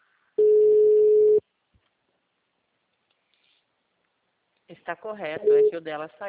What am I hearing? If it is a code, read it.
Ringback tone (Europe, 425 Hz), with speech.
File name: out- -2003- - - .11760